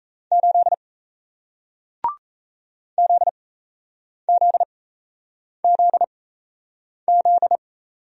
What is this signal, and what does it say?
Morse code, heard as a keyed tone, about 38 wpm, spaced as for 8 wpm, 700 Hz. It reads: 8 ZZZZ